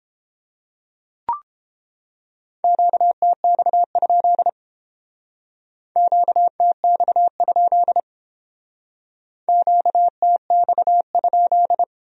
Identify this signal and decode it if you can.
Morse code, heard as a keyed tone, about 30 wpm, 700 Hz. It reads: QTX? QTX? QTX?